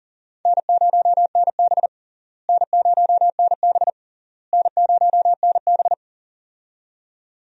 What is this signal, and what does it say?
Morse code, heard as a keyed tone, about 40 wpm, 700 Hz. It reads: N0NB N0NB N0NB
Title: USA Call Signs - 4 Characters - Repeated 3x - 40wpm